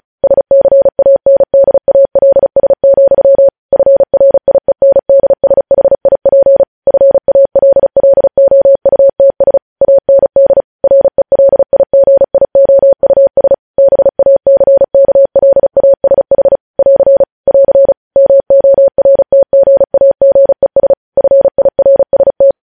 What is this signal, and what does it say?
Morse code, heard as a keyed tone, about 36 wpm, 550 Hz. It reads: SCANDALS, FRIENDSHIP FALLOUTS AND RELIGIOUS BACKLASH <AR> <AR> MORTGAGES FIRST
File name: podcast / News headlines-1x-35x35